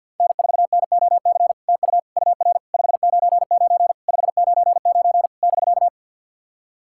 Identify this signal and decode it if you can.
Morse code, heard as a keyed tone, about 50 wpm, 700 Hz. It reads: N4NOC TU UR 599 599 <BK>
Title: SOTA QSO Practice 50wpm